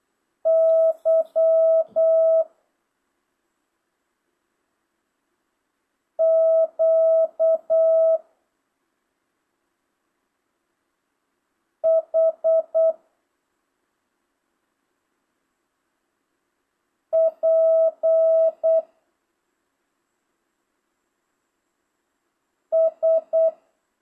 Morse code, heard as a keyed tone, about 8 wpm, 650 Hz. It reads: Y Q H P S
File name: jh8cbh / 2025cw